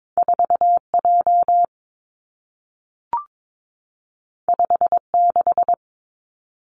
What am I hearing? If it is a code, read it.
Morse code, heard as a keyed tone, about 22 wpm, 700 Hz. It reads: 4 J 5 6
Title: Two Challenging Letter-Number Combinations 22wpm